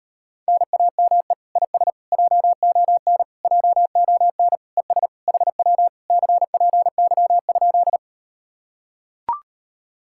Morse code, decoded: NAME IS JON JON ES HW CPY?

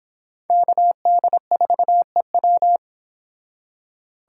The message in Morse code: KD4EW

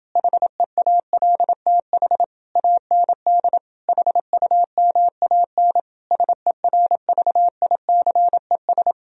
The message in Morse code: HEALTH AND HUMAN SERVICES

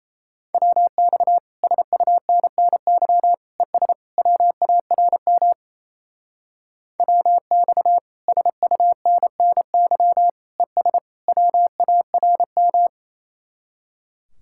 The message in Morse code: WX SUNNY ES WARM WX SUNNY ES WARM

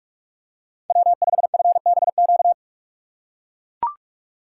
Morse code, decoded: W5FBQ

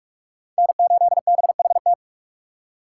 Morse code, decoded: N8BHT

